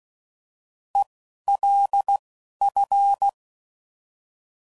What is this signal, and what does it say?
Morse code, heard as a keyed tone, about 16 wpm, spaced as for 11 wpm, 800 Hz. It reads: ELF
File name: Prefixes / By First Letter / Es